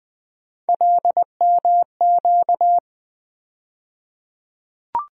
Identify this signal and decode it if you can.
Morse code, heard as a keyed tone, about 20 wpm, 700 Hz. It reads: LMQ
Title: USA Call Sign Postfixes - 3 Letters 20wpm